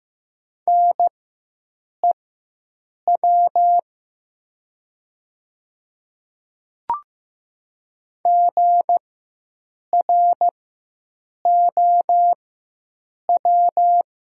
Morse code, heard as a keyed tone, about 15 wpm, spaced as for 7 wpm, 700 Hz. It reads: NEW GROW
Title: 4x Character Spacing - Top 300 Words - 15wpm